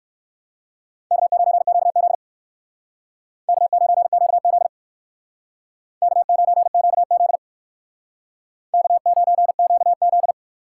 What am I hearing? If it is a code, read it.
Morse code, heard as a keyed tone, about 59 wpm, 700 Hz. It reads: K9QZ K9QZ K9QZ K9QZ